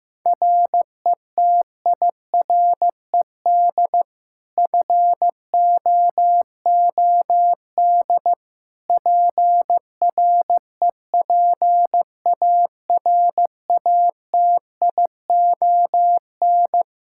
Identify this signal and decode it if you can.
Morse code, heard as a keyed tone, about 16 wpm, 700 Hz. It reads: RETIRED FOOD PREPARATION
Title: K1USN SST QSO Practice 15wpm